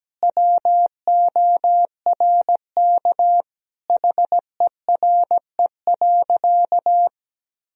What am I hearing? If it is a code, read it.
Morse code, heard as a keyed tone, about 17 wpm, 700 Hz. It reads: WORK HERE.
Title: Sentences from Top 1000 Words - Review Part 4 - 17wpm